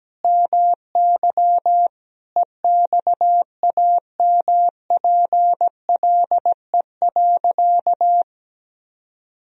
Morse code, heard as a keyed tone, about 17 wpm, 700 Hz. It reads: MY EXAMPLE.
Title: Sentences from Top 1000 Words - Review Part 1 - 17wpm